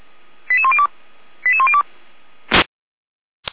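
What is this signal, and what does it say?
Five-tone selective call (ZVEI), heard as quick runs of five tones, tones of about 70 ms. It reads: 88102 88102
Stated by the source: ZVEI five-tone selective call received on SDR.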